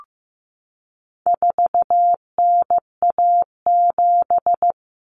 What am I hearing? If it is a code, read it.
Morse code, heard as a keyed tone, about 15 wpm, 700 Hz. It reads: 4NA7
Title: Common Cut Numbers - 4-Digit 15wpm